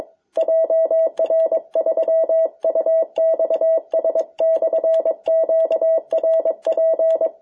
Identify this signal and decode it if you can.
Morse code, heard as a keyed tone, about 22 wpm, 650 Hz. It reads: JR3UXS/QRP
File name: jh8cbh / 2021cw